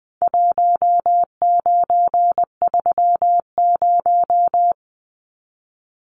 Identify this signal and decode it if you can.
Morse code, heard as a keyed tone, about 20 wpm, 700 Hz. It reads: 1930
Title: Numbers - Contest Style - 4-Digit 20wpm